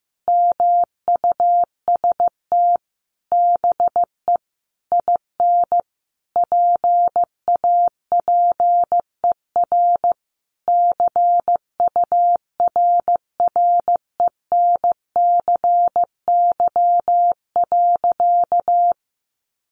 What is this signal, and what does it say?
Morse code, heard as a keyed tone, about 15 wpm, 700 Hz. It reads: MUST BE IN PAPER CURRENCY.
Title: Sentences from Top 1500 Words 15wpm